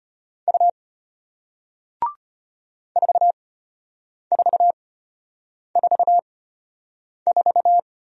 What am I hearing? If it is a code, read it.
Morse code, heard as a keyed tone, about 34 wpm, spaced as for 8 wpm, 700 Hz. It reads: U 4444